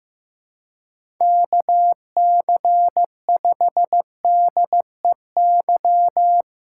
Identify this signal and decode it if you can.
Morse code, heard as a keyed tone, about 15 wpm, 700 Hz. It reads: KC5DEY
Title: USA Call Signs - 6 Characters - 4x Character Spacing - 15wpm